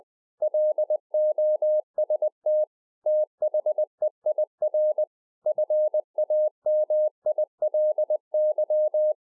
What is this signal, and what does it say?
Morse code, heard as a keyed tone, about 20 wpm, 610 Hz. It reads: LOST THEIR FAMILY